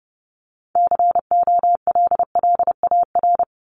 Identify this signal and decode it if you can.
Morse code, heard as a keyed tone, about 30 wpm, 700 Hz. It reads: COLLAR
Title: Top 1751-2000 Words 30wpm